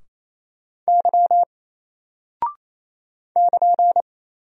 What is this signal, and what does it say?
Morse code, heard as a keyed tone, about 27 wpm, 700 Hz. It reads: Y <KN>
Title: Single Character (Letter:Number:Punctuation:Prosign) 28wpm